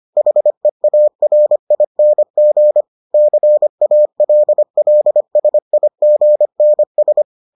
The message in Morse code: HEARING CALLSIGNS

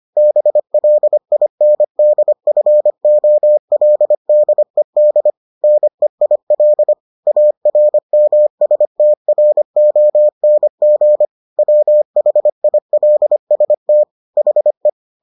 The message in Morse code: BLINDFOLDED NEIL ARMSTRONG WHILST HE

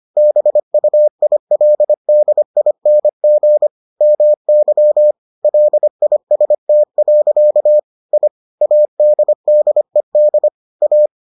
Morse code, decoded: BUILDING MY LIST. I ADDED A

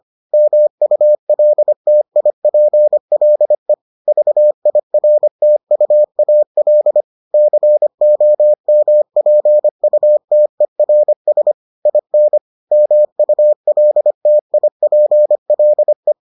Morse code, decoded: MULTIPLE VIRTUAL COMPUTERS IN MULTIPLE

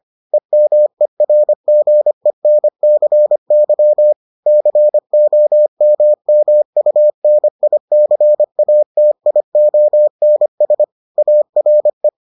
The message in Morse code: EMERGENCY COMMUNICATIONS ARE

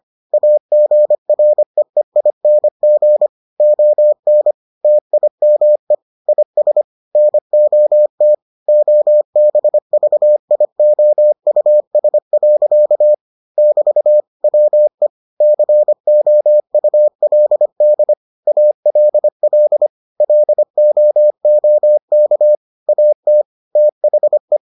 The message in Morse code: AGREEING ON TIME IS NOT OBVIOUS. <BT> WE COULD ALL LOOK AT THE

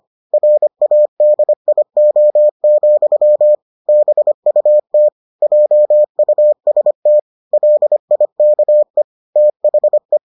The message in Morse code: RADIO, BUT JUST LIKE THE

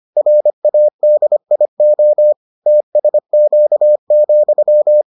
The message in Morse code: RADIO TSQ,

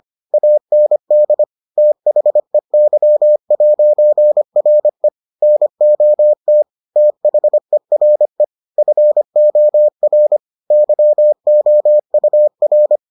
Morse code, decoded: AND THEY'RE NOT THERE FOR YOUR

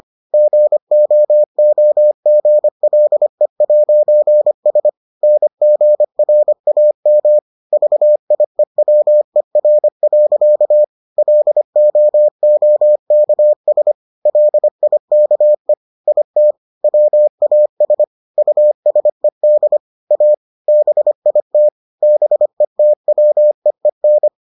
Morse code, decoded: GOOGLE'S NGRAM VIEWER. LOOKS LIKE IT WAS USED A BIT BETWEEN

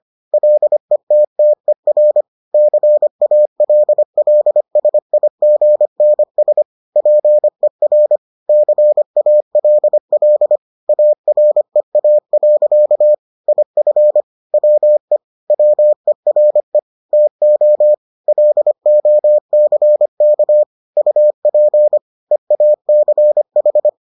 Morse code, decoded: LETTER CALLSIGNS PER CALL AREA. IF WE WERE TO LOCK UP EACH

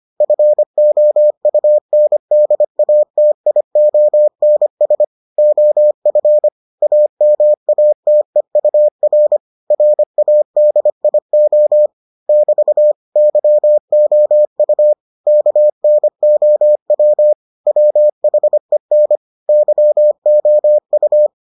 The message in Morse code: FOUNDATIONS OF AMATEUR RADIO <BT> YOU KNOW WHEN YOU

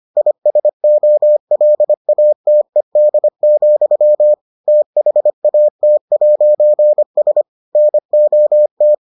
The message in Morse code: ISOLATED, THAT'S NOT

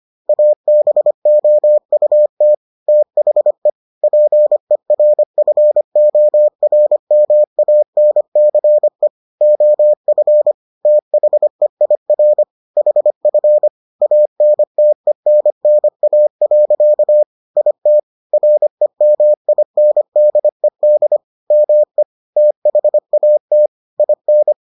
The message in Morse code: ABOUT THE PERFORMANCE OF THEIR HF ANTENNA. IT REMINDED ME THAT IN

